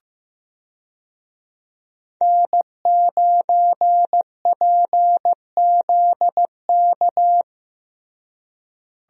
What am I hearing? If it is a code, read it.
Morse code, heard as a keyed tone, about 15 wpm, 700 Hz. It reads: N9PZK
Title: USA Call Signs - 5 Characters - Repeated 3x - 15wpm